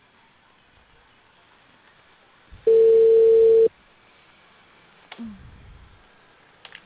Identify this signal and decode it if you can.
Ringback tone (China, 450 Hz).